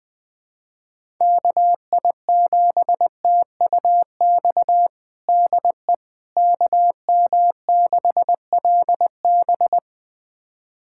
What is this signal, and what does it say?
Morse code, heard as a keyed tone, about 20 wpm, 700 Hz. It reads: KI7TUX DE KM6LB
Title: USA Callsign DE Callsign 20wpm